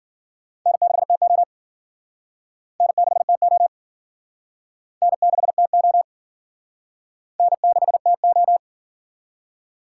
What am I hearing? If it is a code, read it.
Morse code, heard as a keyed tone, about 52 wpm, 700 Hz. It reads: N6TO N6TO N6TO N6TO